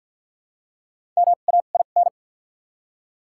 Morse code, heard as a keyed tone, about 48 wpm, 700 Hz. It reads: M A I N
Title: 2x Character Spacing - Top 500 Words - 50wpm